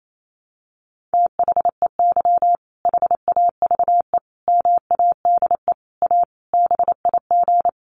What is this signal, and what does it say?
Morse code, heard as a keyed tone, about 28 wpm, 700 Hz. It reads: THEY HAVE MADE A BIG
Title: Sentences from Top 1200 Words 28wpm